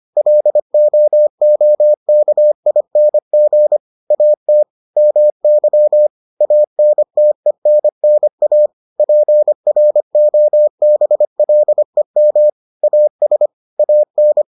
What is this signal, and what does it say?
Morse code, heard as a keyed tone, about 25 wpm, 600 Hz. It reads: LOOKING AT MY ANTENNA PROBLEM AS AN